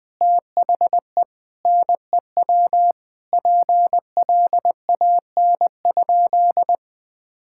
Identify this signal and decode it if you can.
Morse code, heard as a keyed tone, about 21 wpm, 700 Hz. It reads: THE NEW PLAN?
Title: Sentences with 5-Word Phrases - 20wpm